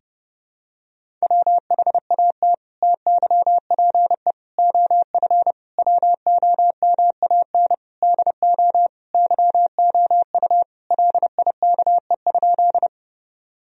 Morse code, heard as a keyed tone, about 32 wpm, 700 Hz. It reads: WHAT TYPE OF WOMAN DO YOU LIKE?
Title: Sentences from Top 1000 Words - Review Part 3 - 30wpm